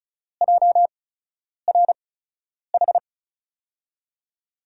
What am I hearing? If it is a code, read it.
Morse code, heard as a keyed tone, about 34 wpm, spaced as for 10 wpm, 700 Hz. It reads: JRH
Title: USA Call Signs - 5 Characters - 8x Character Spacing - 35wpm